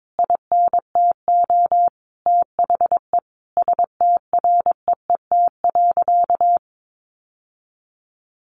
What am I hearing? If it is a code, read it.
Morse code, heard as a keyed tone, about 22 wpm, 700 Hz. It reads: INTO THE STREET.
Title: Sentences from Top 600 Words 22wpm